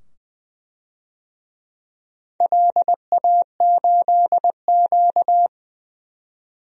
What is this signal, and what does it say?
Morse code, heard as a keyed tone, about 20 wpm, 700 Hz. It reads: LA8Q
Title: DXCC-100 Call Signs - 4 Characters - 20wpm